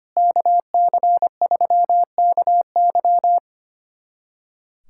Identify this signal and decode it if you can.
Morse code, heard as a keyed tone, about 25 wpm, 700 Hz. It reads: KC3KY
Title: USA Callsign DE Callsign 25wpm